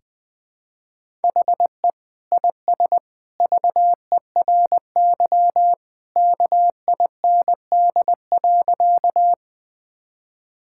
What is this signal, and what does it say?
Morse code, heard as a keyed tone, about 20 wpm, 700 Hz. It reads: HE IS VERY KIND.
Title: Sentences with 2-Word Phrases - 20wpm